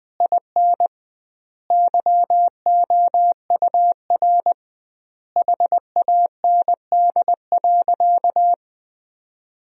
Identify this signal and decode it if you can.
Morse code, heard as a keyed tone, about 20 wpm, 700 Hz. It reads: IN YOUR HAND.
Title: 2x Word-Spacing - Sentences from Top 400 Words - 20wpm